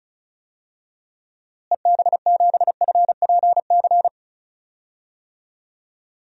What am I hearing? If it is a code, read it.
Morse code, heard as a keyed tone, about 34 wpm, 700 Hz. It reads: EB7FPC
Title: DXCC-100 Call Signs - 6 Characters - Repeated 3x - 35wpm